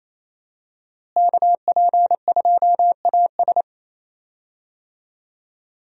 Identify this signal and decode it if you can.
Morse code, heard as a keyed tone, about 27 wpm, 700 Hz. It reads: KP2AS
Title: USA Call Signs - 5 Characters - Repeated 3x - 28wpm